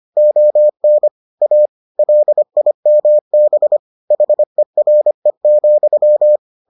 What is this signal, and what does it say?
Morse code, heard as a keyed tone, about 25 wpm, 600 Hz. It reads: ON A LIMB HERE,